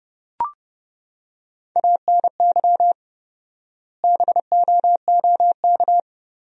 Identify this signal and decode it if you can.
Morse code, heard as a keyed tone, about 32 wpm, 700 Hz. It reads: ANY BOOK
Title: 4x Word-Spacing - Sentences from Top 300 Words - 30wpm